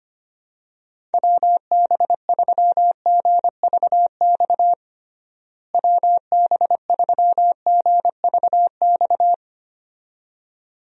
Morse code, heard as a keyed tone, about 25 wpm, 700 Hz. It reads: WB3GVX WB3GVX
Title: USA Call Signs - 6 Characters - Repeated 2x - 25wpm